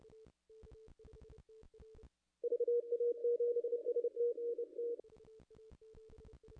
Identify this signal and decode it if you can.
Morse code, heard as a keyed tone, about 29 wpm, 480 Hz. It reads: VA7ST NT
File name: contests / ss-cw-k5tr-op-n5rz -0100